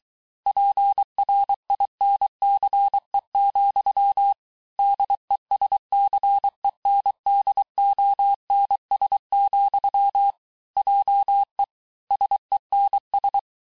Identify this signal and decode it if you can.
Morse code, heard as a keyed tone, about 23 wpm, 800 Hz. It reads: PRINCE, DESCENDONS, JE SENS
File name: cours-du-mercredi / audios / COURS-855--texte-clair-vitesse-22-mpm